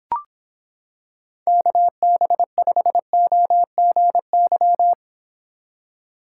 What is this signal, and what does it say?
Morse code, heard as a keyed tone, about 26 wpm, 700 Hz. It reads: KB5OGY